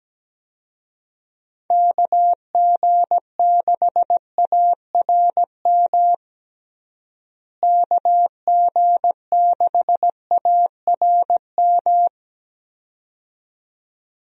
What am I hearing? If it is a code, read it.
Morse code, heard as a keyed tone, about 17 wpm, 700 Hz. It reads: KG6ARM KG6ARM
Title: USA Call Signs - 6 Characters - Repeated 2x - 17wpm